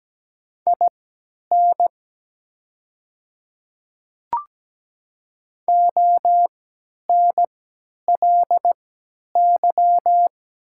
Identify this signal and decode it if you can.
Morse code, heard as a keyed tone, about 17 wpm, spaced as for 10 wpm, 700 Hz. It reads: IN ONLY
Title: 3x Character Spacing - Top 200 Words - 17wpm